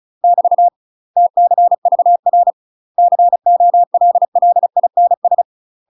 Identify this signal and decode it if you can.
Morse code, heard as a keyed tone, about 34 wpm, 700 Hz. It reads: <BT> TCVR COLLINS
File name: UK MASTER QSOs TRAINING / 35WPM / UK MASTER QSOs 0001